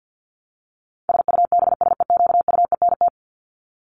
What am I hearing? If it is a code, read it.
Morse code, heard as a keyed tone, about 50 wpm, 700 Hz. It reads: SUBSEQUENT
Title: LY Suffix - Top 100 Words with Base Words 50wpm